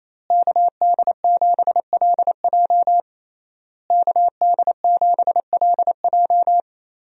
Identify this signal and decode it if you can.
Morse code, heard as a keyed tone, about 27 wpm, 700 Hz. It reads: KD7LJ KD7LJ
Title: USA Call Signs - 5 Characters - Repeated 3x - 28wpm